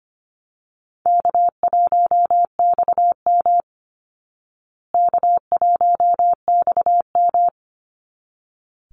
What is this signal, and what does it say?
Morse code, heard as a keyed tone, about 25 wpm, 700 Hz. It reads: K1XM K1XM
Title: Call Signs Repeated 3x - 25wpm